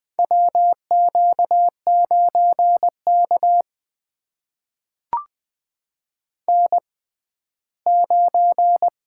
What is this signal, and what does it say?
Morse code, heard as a keyed tone, about 20 wpm, spaced as for 14 wpm, 700 Hz. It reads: WQ9K N9
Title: USA Call Signs - 4 Characters - 6x Character Spacing - 20wpm